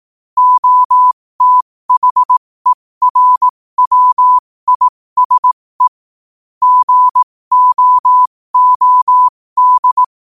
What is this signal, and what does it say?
Morse code, heard as a keyed tone, about 18 wpm, spaced as for 15 wpm, 1000 Hz. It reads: OTHERWISE GOOD
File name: qotdfiles / 15wpm